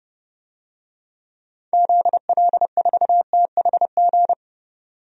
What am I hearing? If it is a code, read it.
Morse code, heard as a keyed tone, about 29 wpm, 700 Hz. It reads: ZL4THG
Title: DXCC-100 Call Signs - 6 Characters - 30wpm